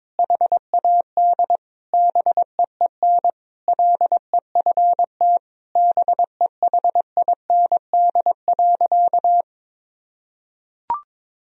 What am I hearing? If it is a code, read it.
Morse code, heard as a keyed tone, about 22 wpm, 700 Hz. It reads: HAD BEEN LEFT BEHIND.
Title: Sentences from Top 1000 Words - Review Part 1 - 22wpm